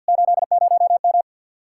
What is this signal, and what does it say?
Morse code, heard as a keyed tone, about 64 wpm, 700 Hz. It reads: 80M